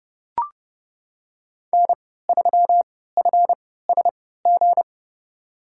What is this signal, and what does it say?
Morse code, heard as a keyed tone, about 30 wpm, spaced as for 17 wpm, 700 Hz. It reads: N3FSG
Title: USA Call Signs - 5 Characters - 3x Character Spacing - 30wpm